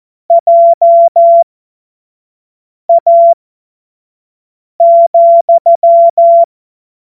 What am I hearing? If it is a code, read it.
Morse code, heard as a keyed tone, about 14 wpm, spaced as for 5 wpm, 680 Hz. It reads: JA,
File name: Morsekurs / Witze / Witz 13. Juli 2021, 25 F70